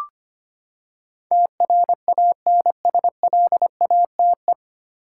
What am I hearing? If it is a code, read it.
Morse code, heard as a keyed tone, about 25 wpm, 700 Hz. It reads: TRANSLATE
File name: Nine Letter Words - 25wpm